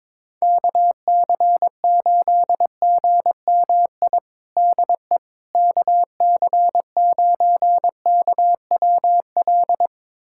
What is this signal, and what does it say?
Morse code, heard as a keyed tone, about 22 wpm, 700 Hz. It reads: KC8GMI DE KC9KWL
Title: USA Callsign DE Callsign - 6 Characters - 22wpm